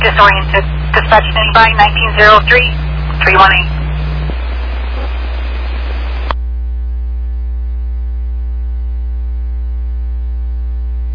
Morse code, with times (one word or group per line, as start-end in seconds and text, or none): none